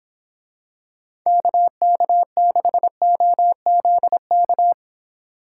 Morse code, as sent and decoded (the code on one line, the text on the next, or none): -.- -.- -.... --- --.. -.-
KK6OZK